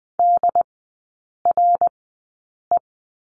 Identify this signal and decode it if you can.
Morse code, heard as a keyed tone, about 20 wpm, spaced as for 8 wpm, 700 Hz. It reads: DRE
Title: 218 Lesson 35e - Number 0 - Three Characters 20wpm